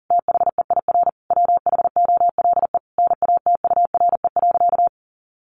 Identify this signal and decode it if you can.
Morse code, heard as a keyed tone, about 40 wpm, 700 Hz. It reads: THEIR WHOLE NATURE.
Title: Sentences from Top 1500 Words 40wpm